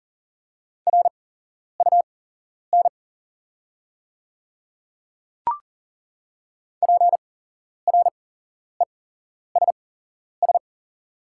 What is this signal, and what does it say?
Morse code, heard as a keyed tone, about 39 wpm, spaced as for 11 wpm, 690 Hz. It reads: RUN PRESS
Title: 8x Character Spacing - Top 300 Words - 40wpm